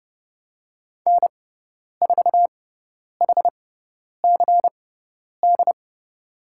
Morse code, heard as a keyed tone, about 30 wpm, spaced as for 10 wpm, 690 Hz. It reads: N4HCD
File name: Lesson 18g - Letter m - Call Signs 30wpm